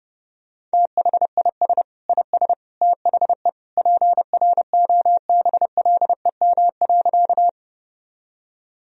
Morse code, heard as a keyed tone, about 32 wpm, 700 Hz. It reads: THIS IS THE PROBLEM.
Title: Sentences from Top 1000 Words - Review Part 2 - 30wpm